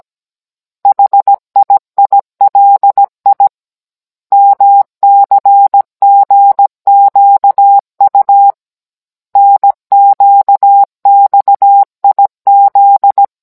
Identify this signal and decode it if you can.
Morse code, heard as a keyed tone, about 17 wpm, 800 Hz. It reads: HIILI MCGQU NQXIZ
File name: Cours-NR19-16mm